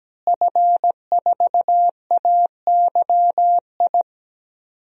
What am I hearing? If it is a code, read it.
Morse code, heard as a keyed tone, about 17 wpm, 700 Hz. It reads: F4AYI
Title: DXCC-100 Call Signs - 5 Characters - Repeated 2x - 17wpm